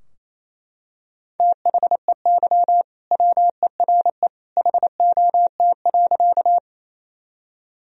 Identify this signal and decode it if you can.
Morse code, heard as a keyed tone, about 28 wpm, 700 Hz. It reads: THEY WERE HOT.
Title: Sentences from Top 1000 Words - Review Part 3 - 28wpm